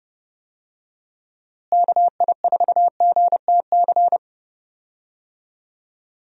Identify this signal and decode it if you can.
Morse code, heard as a keyed tone, about 30 wpm, 700 Hz. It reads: KI4GTC
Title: USA Call Signs - 6 Characters - Repeated 2x - 30wpm